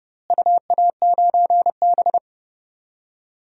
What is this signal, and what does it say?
Morse code, heard as a keyed tone, about 30 wpm, 700 Hz. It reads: UA9B
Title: DXCC-100 Call Signs - 4 Characters - Repeated 3x - 30wpm